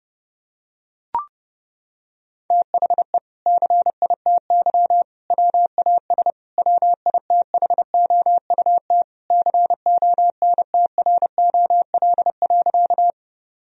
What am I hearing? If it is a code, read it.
Morse code, heard as a keyed tone, about 30 wpm, 700 Hz. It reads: THE CITY WAS WITHOUT CONTROL.
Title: Sentences from Top 1000 Words - Review Part 3 - 30wpm